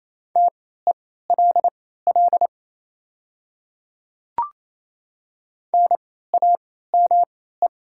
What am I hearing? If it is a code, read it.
Morse code, heard as a keyed tone, about 27 wpm, spaced as for 16 wpm, 700 Hz. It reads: TELL NAME
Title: 3x Character Spacing - Top 200 Words - 28wpm